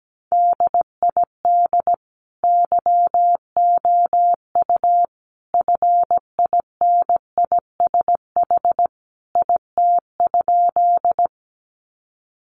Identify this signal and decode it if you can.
Morse code, heard as a keyed tone, about 17 wpm, 700 Hz. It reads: DID YOU FINISH IT?
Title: Sentences from Top 1200 Words 17wpm